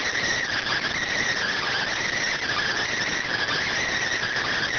Emergency Alert System (SAME) tones, heard as harsh buzzing bursts. It is heard from 0.5 to 3.8 s.